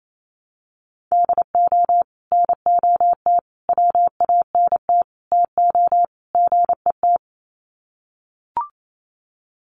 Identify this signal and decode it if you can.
Morse code, heard as a keyed tone, about 28 wpm, 700 Hz. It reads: DO NOT WANT TO GET
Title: Sets of 5 Words 28wpm